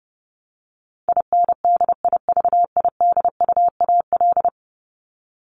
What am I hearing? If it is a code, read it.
Morse code, heard as a keyed tone, about 30 wpm, 700 Hz. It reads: INDIVIDUAL
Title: ES:S Suffix - Top 100 Words with Base Words 30wpm